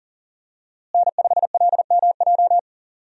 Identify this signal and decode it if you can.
Morse code, heard as a keyed tone, about 40 wpm, 700 Hz. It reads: N5LMJ
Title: USA Call Signs - 5 Characters - Repeated 3x - 40wpm